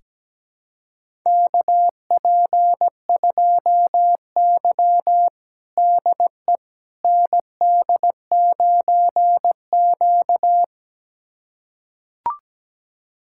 Morse code, heard as a keyed tone, about 17 wpm, 700 Hz. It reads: KP2Y DE ND9Q
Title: USA Callsign DE Callsign - 4 Characters - 17wpm